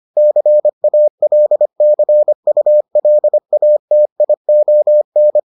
Morse code, heard as a keyed tone, about 25 wpm, 600 Hz. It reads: CALCULATION